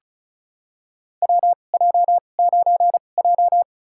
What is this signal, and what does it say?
Morse code, heard as a keyed tone, about 36 wpm, spaced as for 25 wpm, 700 Hz. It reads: WJ9J
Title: USA Call Signs - 4 Characters - 2x Character Spacing - 35wpm